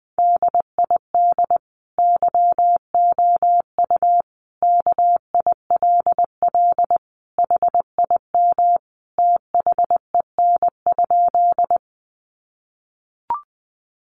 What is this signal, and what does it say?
Morse code, heard as a keyed tone, about 20 wpm, 700 Hz. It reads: DID YOU KILL HIM THEN?
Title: Sentences from Top 1400 Words 20wpm